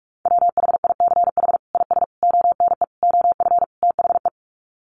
Morse code, heard as a keyed tone, about 44 wpm, 700 Hz. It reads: WHICH IS ONE OF THE
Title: Sets of 5 Words Encore 45wpm